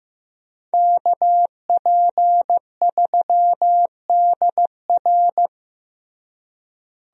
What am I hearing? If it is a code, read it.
Morse code, heard as a keyed tone, about 15 wpm, 700 Hz. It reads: KP3DR
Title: USA Call Signs - 5 Characters - Repeated 2x - 15wpm